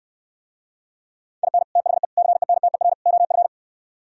Morse code, heard as a keyed tone, about 67 wpm, 700 Hz. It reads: IN THE CENTER OF